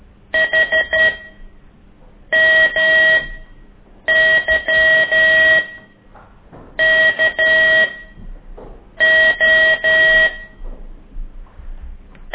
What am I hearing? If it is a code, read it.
Morse code, heard as a keyed tone, about 12 wpm, spaced as for 6 wpm, 1900 Hz. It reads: HMYKO
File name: ik1weg2 / CORSI CW